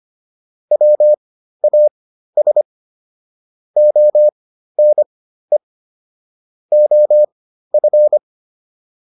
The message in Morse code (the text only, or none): WAS ONE OF